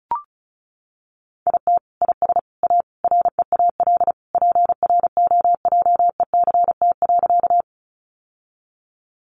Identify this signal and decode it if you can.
Morse code, heard as a keyed tone, about 34 wpm, 700 Hz. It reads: IT IS A REAL PROJECT.
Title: Sentences from Top 1400 Words 35wpm